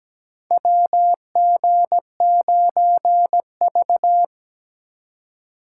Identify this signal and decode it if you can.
Morse code, heard as a keyed tone, about 17 wpm, 700 Hz. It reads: WG9V
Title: USA Call Signs - 3x Character Spacing - 17wpm